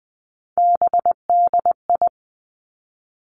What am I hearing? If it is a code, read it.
Morse code, heard as a keyed tone, about 19 wpm, 700 Hz. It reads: BDI
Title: Three Challenging Letter-Number Combinations 20wpm